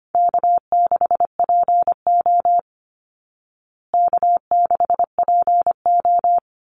Morse code, heard as a keyed tone, about 25 wpm, 700 Hz. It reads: K6PO K6PO
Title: Call Signs Repeated 3x - 25wpm